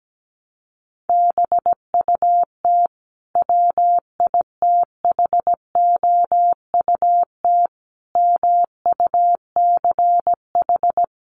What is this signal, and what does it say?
Morse code, heard as a keyed tone, about 17 wpm, 700 Hz. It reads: BUT WITHOUT MUCH
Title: Sentences from Top 900 Words 17wpm